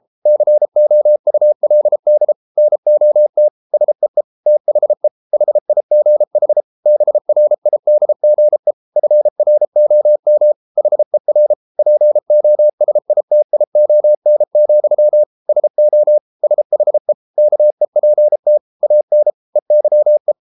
Morse code, heard as a keyed tone, about 33 wpm, 600 Hz. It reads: COULD NOT SEE THE HIGH BRIDGE FROM HER POSITION, SO SHE KEPT AN EYE